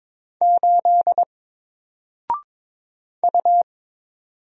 Morse code, heard as a keyed tone, about 22 wpm, 700 Hz. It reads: 8 U
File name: Single-Letter-Number-22wpm